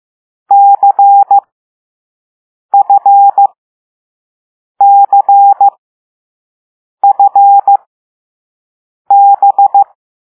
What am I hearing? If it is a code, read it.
Morse code, heard as a keyed tone, about 15 wpm, spaced as for 5 wpm, 800 Hz. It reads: CFCFB
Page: Play Lesson 6 C - all letters in previous lessons